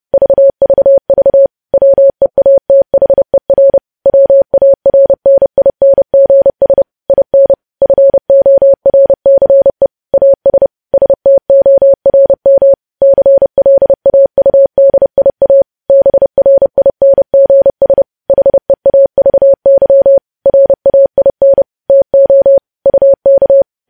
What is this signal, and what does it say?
Morse code, heard as a keyed tone, about 30 wpm, 550 Hz. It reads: VVV WEATHER WARNINGS IN FORCE AS STORM CLAUDIA BRINGS HEAVY RAIN TO UK
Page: News Headlines in Morse Code at 30 WPM